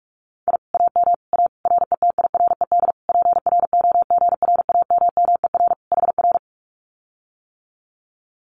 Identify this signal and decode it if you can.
Morse code, heard as a keyed tone, about 46 wpm, 700 Hz. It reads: I AM A RETIRED PROGRAMMER HR
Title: Common QSO Phrases Encore 45wpm